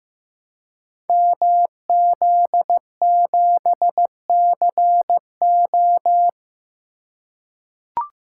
Morse code, heard as a keyed tone, about 15 wpm, 700 Hz. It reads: MZ7CO
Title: Code Groups - 5 Letter:Numbers - 15wpm